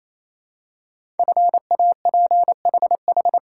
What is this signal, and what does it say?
Morse code, heard as a keyed tone, about 27 wpm, 700 Hz. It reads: FAPHH
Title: Code Groups - 5 Letters - 28wpm